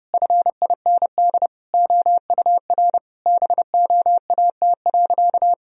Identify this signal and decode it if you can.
Morse code, heard as a keyed tone, about 30 wpm, 700 Hz. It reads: FIND OUR BOAT.
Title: Sentences from Top 1000 Words - Review Part 4 - 30wpm